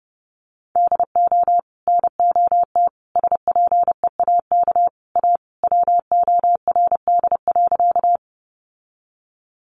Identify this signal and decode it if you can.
Morse code, heard as a keyed tone, about 30 wpm, 700 Hz. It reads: DO NOT SPEAK A WORD.
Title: Sentences from Top 1100 Words 30wpm